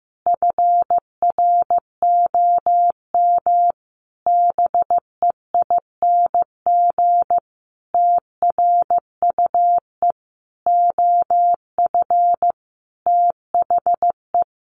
Morse code, heard as a keyed tone, about 15 wpm, 700 Hz. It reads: FROM BEING TRUE OF THE
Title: Sentences from Top 1400 Words 15wpm